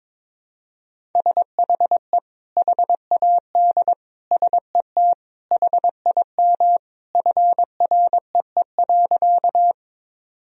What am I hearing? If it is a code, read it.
Morse code, heard as a keyed tone, about 22 wpm, 700 Hz. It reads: SHE HAD SET HIM FREE.
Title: Sentences from Top 1000 Words - Review Part 1 - 22wpm